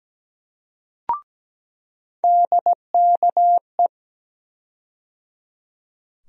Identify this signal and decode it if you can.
Morse code, heard as a keyed tone, about 17 wpm, 700 Hz. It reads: DKE
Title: USA Call Sign Postfixes - 3 Letters 17wpm